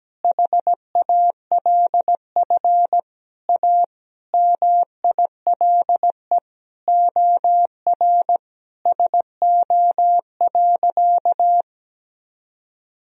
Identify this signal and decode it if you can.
Morse code, heard as a keyed tone, about 17 wpm, 700 Hz. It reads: HALF A MILE OR SO.
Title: Sentences from Top 1000 Words - Review Part 1 - 17wpm